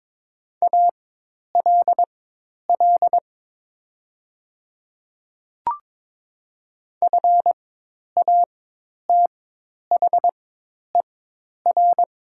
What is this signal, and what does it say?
Morse code, heard as a keyed tone, about 22 wpm, spaced as for 10 wpm, 700 Hz. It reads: ALL FATHER
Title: 4x Character Spacing - Top 300 Words - 22wpm